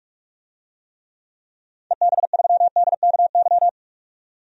Morse code, heard as a keyed tone, about 47 wpm, 700 Hz. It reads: EB3DKY
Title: DXCC-100 Call Signs - 6 Characters - 45wpm